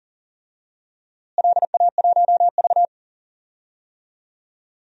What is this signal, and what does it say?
Morse code, heard as a keyed tone, about 39 wpm, 700 Hz. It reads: LA1V
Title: DXCC-100 Call Signs - 4 Characters - Repeated 2x - 40wpm